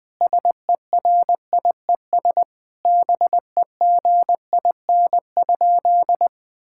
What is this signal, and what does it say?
Morse code, heard as a keyed tone, about 21 wpm, 700 Hz. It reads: SERIES BEGIN?
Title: Sentences from Top 1000 Words - Review Part 5 - 20wpm